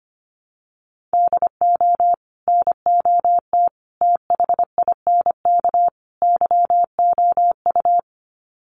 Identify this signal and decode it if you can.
Morse code, heard as a keyed tone, about 25 wpm, 700 Hz. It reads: DO NOT THINK YOU
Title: Sets of 4 Words 25wpm